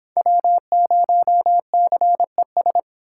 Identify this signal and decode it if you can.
Morse code, heard as a keyed tone, about 31 wpm, 700 Hz. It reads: W0CES